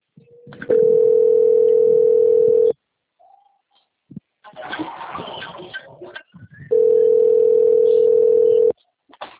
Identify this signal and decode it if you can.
Ringback tone (North America), 440 + 480 Hz.